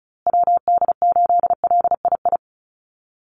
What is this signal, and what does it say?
Morse code, heard as a keyed tone, about 34 wpm, 700 Hz. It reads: WD8LII
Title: USA Call Signs Repeated 2x 35wpm